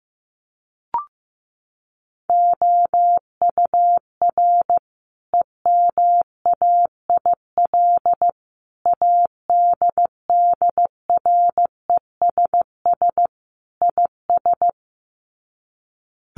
Morse code, heard as a keyed tone, about 15 wpm, 700 Hz. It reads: OUR EMAIL ADDRESS IS
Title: Sets of 5 Words Encore2 15wpm